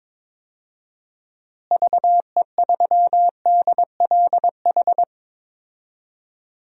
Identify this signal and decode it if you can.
Morse code, heard as a keyed tone, about 22 wpm, 700 Hz. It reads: VE3DLH
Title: DXCC-100 Call Signs - 6 Characters - Repeated 2x - 22wpm